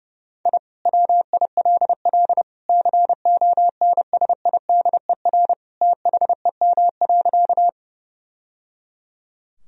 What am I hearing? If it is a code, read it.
Morse code, heard as a keyed tone, about 32 wpm, 700 Hz. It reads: I WILL CONSIDER THEM.
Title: Sentences from Top 1000 Words - Review Part 4 - 30wpm